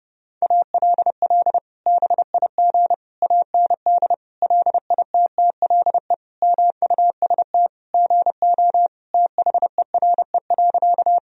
Morse code, heard as a keyed tone, about 32 wpm, 700 Hz. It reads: ALL BIG AND LITTLE MUST GO THERE.
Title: Sentences from Top 1000 Words - Review Part 1 - 30wpm